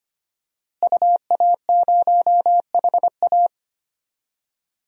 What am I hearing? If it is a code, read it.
Morse code, heard as a keyed tone, about 25 wpm, 700 Hz. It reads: UA0HA
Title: DXCC-100 Call Signs - 5 Characters - Repeated 3x - 25wpm